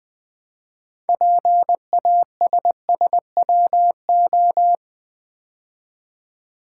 Morse code, heard as a keyed tone, about 20 wpm, 700 Hz. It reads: PASSWO
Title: Build A Word - 2 Letters At A Time - 8-Letter Words - 20wpm